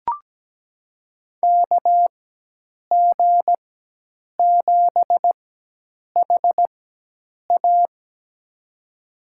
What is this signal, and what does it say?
Morse code, heard as a keyed tone, about 17 wpm, spaced as for 8 wpm, 700 Hz. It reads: KG7HA
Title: USA Call Signs - 5 Characters - 4x Character Spacing - 17wpm